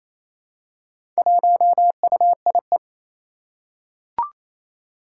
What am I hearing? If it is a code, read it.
Morse code, heard as a keyed tone, about 28 wpm, 700 Hz. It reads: 1UIE
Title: Four Challenging Letter-Number Combinations 28wpm